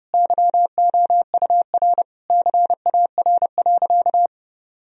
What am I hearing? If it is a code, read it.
Morse code, heard as a keyed tone, about 30 wpm, 700 Hz. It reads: YOUR CAR.
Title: Sentences from Top 1000 Words - Review Part 1 - 30wpm